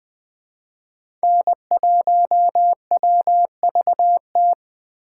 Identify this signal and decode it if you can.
Morse code, heard as a keyed tone, about 20 wpm, 700 Hz. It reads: N1WVT
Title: Speed Racing - USA 1x3 Call Signs 17wpm